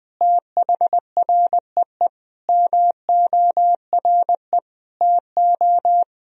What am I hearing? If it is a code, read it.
Morse code, heard as a keyed tone, about 21 wpm, 700 Hz. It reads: THREE MORE TO
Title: Sentences from Top 1000 Words - Review Part 1 - 20wpm